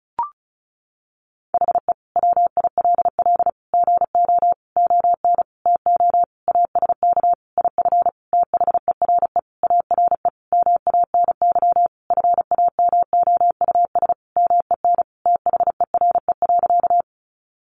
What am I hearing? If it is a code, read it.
Morse code, heard as a keyed tone, about 34 wpm, 700 Hz. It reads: HE WILL GO ON TO ASK IF THERE ARE MANY FAMOUS MEN THERE.
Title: Sentences from Top 1500 Words 35wpm